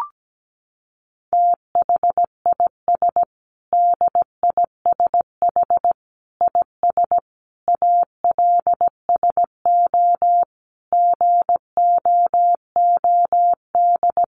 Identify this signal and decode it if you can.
Morse code, heard as a keyed tone, about 17 wpm, 700 Hz. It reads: THIS DISH IS ALSO GOOD
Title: Sentences from Top 1300 Words 17wpm